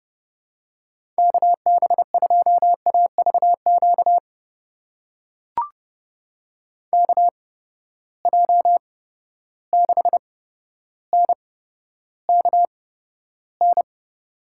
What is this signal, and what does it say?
Morse code, heard as a keyed tone, about 32 wpm, spaced as for 13 wpm, 700 Hz. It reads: KB2AVQ KJ6NKN
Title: USA Call Signs - 6 Characters - 8x Character Spacing - 30wpm